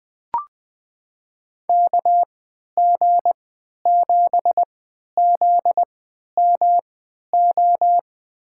Morse code, heard as a keyed tone, about 20 wpm, spaced as for 11 wpm, 700 Hz. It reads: KG7ZMO
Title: USA Call Signs - 6 Characters - 3x Character Spacing - 20wpm